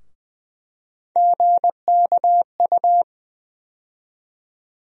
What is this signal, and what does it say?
Morse code, heard as a keyed tone, about 20 wpm, 700 Hz. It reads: GKU
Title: USA Call Sign Postfixes - 3 Letters 20wpm